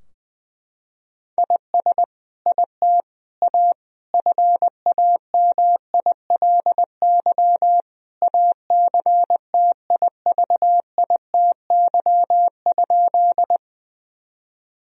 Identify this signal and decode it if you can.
Morse code, heard as a keyed tone, about 20 wpm, 700 Hz. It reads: IS IT A FAMILY ACTIVITY?
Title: Sentences from Top 1000 Words - Review Part 4 - 20wpm